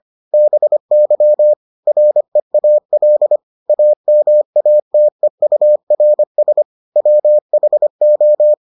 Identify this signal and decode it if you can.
Morse code, heard as a keyed tone, about 25 wpm, 600 Hz. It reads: BY REAL AMATEURS WHO